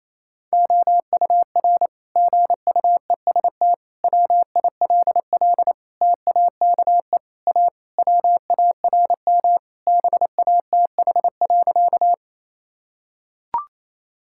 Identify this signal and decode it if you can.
Morse code, heard as a keyed tone, about 27 wpm, 700 Hz. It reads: OUR GUEST WILL TAKE A WARM BATH.
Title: Sentences from Top 1600 Words 28wpm